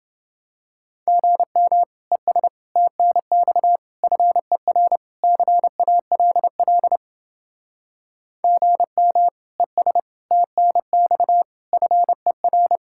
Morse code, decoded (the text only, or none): GM ES TNX FER CALL GM ES TNX FER